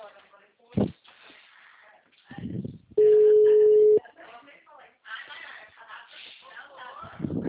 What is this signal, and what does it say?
Ringback tone (Europe, 425 Hz), with speech.
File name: out- -2013- - - .25784